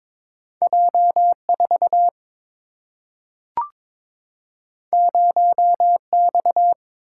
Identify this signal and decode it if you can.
Morse code, heard as a keyed tone, about 22 wpm, 700 Hz. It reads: J 4 0 X
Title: Single-Letter Single-Number 22wpm